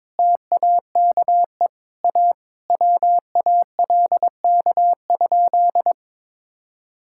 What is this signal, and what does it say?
Morse code, heard as a keyed tone, about 22 wpm, 700 Hz. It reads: TAKE A WALK?
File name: practice / 22wpm / Sentences with 3-Word Phrases - 22wpm